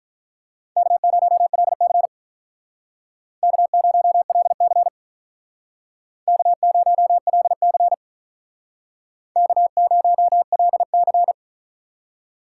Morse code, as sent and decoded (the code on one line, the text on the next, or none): -.- ----- .-.. -.-. / -.- ----- .-.. -.-. / -.- ----- .-.. -.-. / -.- ----- .-.. -.-.
K0LC K0LC K0LC K0LC